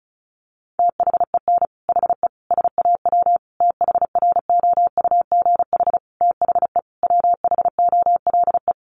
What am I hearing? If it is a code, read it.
Morse code, heard as a keyed tone, about 34 wpm, 700 Hz. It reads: THEN HE SAW THROUGH THE WHOLE
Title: Sentences from Top 400 Words 35wpm